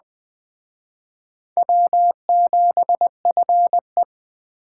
Morse code, heard as a keyed tone, about 21 wpm, 700 Hz. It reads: W7FE
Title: Speed Racing - USA 1x2 Call Signs 17wpm